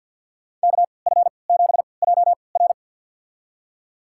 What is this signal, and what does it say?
Morse code, heard as a keyed tone, about 51 wpm, spaced as for 28 wpm, 700 Hz. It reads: KF7JR
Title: USA Call Signs - 5 Characters - 3x Character Spacing - 50wpm